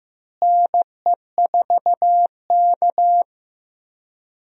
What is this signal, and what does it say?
Morse code, heard as a keyed tone, about 15 wpm, 700 Hz. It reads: NE4K
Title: USA Call Signs - 4 Characters - Repeated 3x - 15wpm